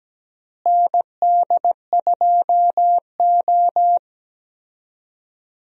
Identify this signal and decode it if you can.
Morse code, heard as a keyed tone, about 17 wpm, 700 Hz. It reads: ND2O
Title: USA Call Signs - 4 Characters - Repeated 2x - 17wpm